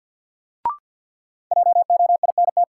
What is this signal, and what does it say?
Morse code, heard as a keyed tone, about 46 wpm, 700 Hz. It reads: JOINT
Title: Sentences from Top 1300 Words - New Word Drill - 50wpm